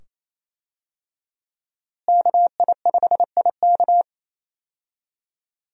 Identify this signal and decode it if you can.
Morse code, heard as a keyed tone, about 28 wpm, 700 Hz. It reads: KI5IK
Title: USA Call Signs - 5 Characters - Repeated 2x - 28wpm